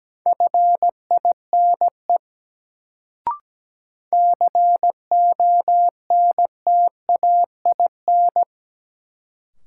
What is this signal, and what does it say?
Morse code, heard as a keyed tone, about 17 wpm, 700 Hz. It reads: FINE CONTAIN
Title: Sentences from Top 500 Words - New Word Drill - 17wpm